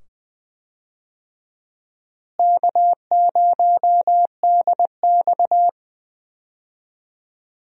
Morse code, heard as a keyed tone, about 20 wpm, 700 Hz. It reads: K0DX
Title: USA Call Signs - 4 Characters - Repeated 2x - 20wpm